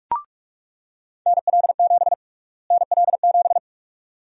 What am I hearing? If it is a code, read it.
Morse code, heard as a keyed tone, about 43 wpm, 700 Hz. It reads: NL7 NL7
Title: USA Call Sign Prefixes - Repeated 2x - 45wpm